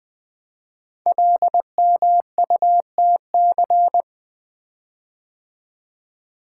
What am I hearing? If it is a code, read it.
Morse code, heard as a keyed tone, about 20 wpm, 700 Hz. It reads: LMUTC
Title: Code Groups - 5 Letters - 20wpm